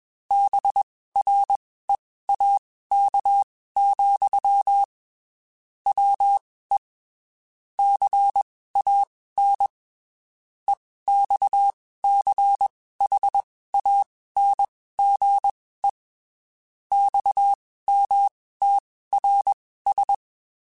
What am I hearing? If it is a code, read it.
Morse code, heard as a keyed tone, about 21 wpm, spaced as for 15 wpm, 800 Hz. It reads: BREAK, WE CAN EXCHANGE XMTRS
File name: (spaced) / QSOs Spaced-mp3s / WA7QOX 1 @20 sp